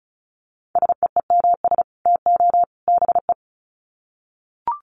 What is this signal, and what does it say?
Morse code, heard as a keyed tone, about 34 wpm, 700 Hz. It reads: SEEMS TO BE
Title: Sets of 3 Words 35wpm